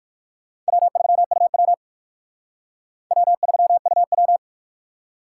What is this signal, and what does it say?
Morse code, heard as a keyed tone, about 51 wpm, 700 Hz. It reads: W3UW W3UW
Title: Speed Racing - USA 1x2 Call Signs 45wpm